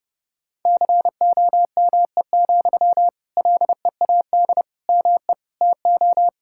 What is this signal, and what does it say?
Morse code, heard as a keyed tone, about 32 wpm, 700 Hz. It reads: COME, LEAD ME TO
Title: Sentences from Top 1600 Words 30wpm